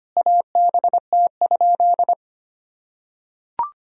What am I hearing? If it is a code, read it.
Morse code, heard as a keyed tone, about 25 wpm, 700 Hz. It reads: ABT?
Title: Sentences from Top 100 Words and Abbreviations 25wpm